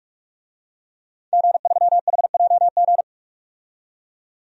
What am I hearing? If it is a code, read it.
Morse code, heard as a keyed tone, about 41 wpm, 700 Hz. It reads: G3HJG
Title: DXCC-100 Call Signs - 5 Characters - 45wpm